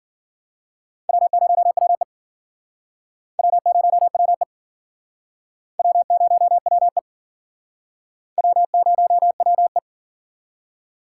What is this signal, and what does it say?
Morse code, heard as a keyed tone, about 52 wpm, 700 Hz. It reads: W0WE W0WE W0WE W0WE